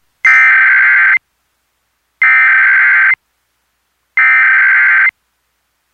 Emergency Alert System (SAME) tones (header bursts), heard as harsh buzzing bursts. It is heard from 0.3 to 5.1 s.
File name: KRBZ-EAS